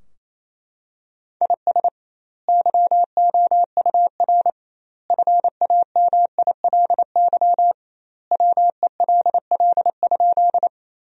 Morse code, heard as a keyed tone, about 28 wpm, 700 Hz. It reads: IS YOUR FAMILY WELL?
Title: 2x Word-Spacing - Sentences from Top 400 Words - 28wpm